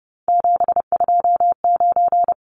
Morse code, heard as a keyed tone, about 30 wpm, 700 Hz. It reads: 729
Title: Numbers - 3-Digit 30wpm